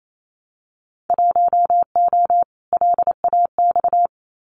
Morse code, heard as a keyed tone, about 28 wpm, 700 Hz. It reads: 1O LAX
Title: Winter Field Day Exchanges 28wpm